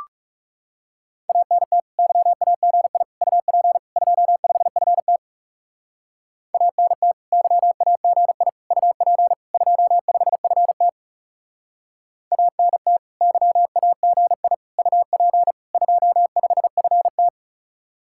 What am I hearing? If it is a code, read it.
Morse code, heard as a keyed tone, about 37 wpm, 700 Hz. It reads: ANT YAGI UP 25FT ANT YAGI UP 25FT ANT YAGI UP 25FT
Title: Speed-Racing Common QSO Phrases - 30wpm